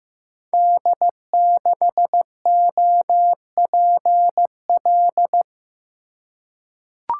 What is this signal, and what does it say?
Morse code, heard as a keyed tone, about 15 wpm, 700 Hz. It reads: D6OPL
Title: Code Groups - 5 Letter:Numbers - 15wpm